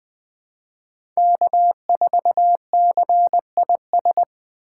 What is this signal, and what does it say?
Morse code, heard as a keyed tone, about 20 wpm, 700 Hz. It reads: K4CIS
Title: USA Call Signs - 5 Characters - 3x Character Spacing - 20wpm